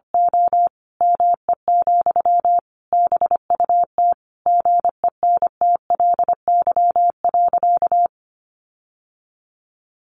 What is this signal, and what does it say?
Morse code, heard as a keyed tone, about 25 wpm, 700 Hz. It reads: O ME, BUT GENTLY.
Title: Sentences from Top 1500 Words 25wpm